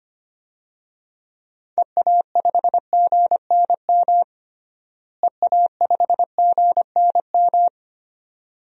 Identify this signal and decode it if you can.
Morse code, heard as a keyed tone, about 25 wpm, 700 Hz. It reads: EA5GNM EA5GNM
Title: DXCC-100 Call Signs - 6 Characters - Repeated 2x - 25wpm